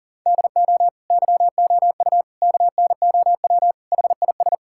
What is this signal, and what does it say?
Morse code, heard as a keyed tone, about 37 wpm, 700 Hz. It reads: DO YOU KNOW HIS
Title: Sentences from Top 1000 Words - Review Part 6 - 40wpm